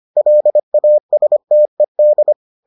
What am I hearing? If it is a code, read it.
Morse code, heard as a keyed tone, about 25 wpm, 600 Hz. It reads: LASTED